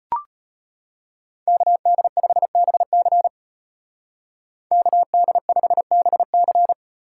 Morse code, heard as a keyed tone, about 33 wpm, 700 Hz. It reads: KD5BC KD5BC